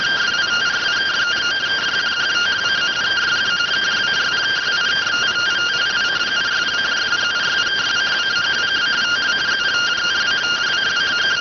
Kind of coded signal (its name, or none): two-tone teleprinter signal